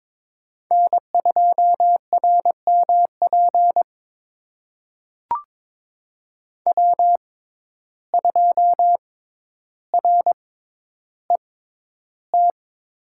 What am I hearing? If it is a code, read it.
Morse code, heard as a keyed tone, about 22 wpm, spaced as for 11 wpm, 700 Hz. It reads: N2RMP W2RET
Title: USA Call Signs - 5 Characters - 6x Character Spacing - 22wpm